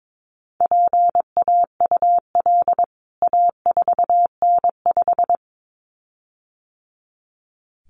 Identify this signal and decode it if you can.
Morse code, heard as a keyed tone, about 22 wpm, 700 Hz. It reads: PAUL A4N5
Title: CWT Exchange 22wpm